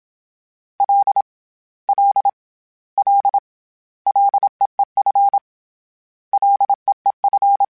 Morse code, heard as a keyed tone, about 27 wpm, 800 Hz. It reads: L L L LEEF LEEF
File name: Short Runs / Short Runs-mp3s / SinglePrefixes @25